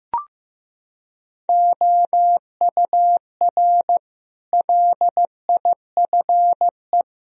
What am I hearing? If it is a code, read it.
Morse code, heard as a keyed tone, about 16 wpm, 700 Hz. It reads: OUR LIFE
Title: Sentences from Top 1000 Words - Review Part 3 - 15wpm